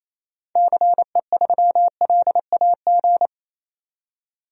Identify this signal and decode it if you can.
Morse code, heard as a keyed tone, about 28 wpm, 700 Hz. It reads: CE3LAG
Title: DXCC-100 Call Signs - 6 Characters - 28wpm